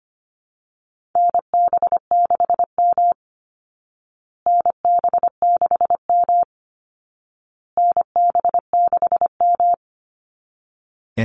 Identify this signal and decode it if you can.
Morse code, heard as a keyed tone, about 25 wpm, 700 Hz. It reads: NB6M NB6M NB6M
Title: Call Signs Repeated 2x - 25wpm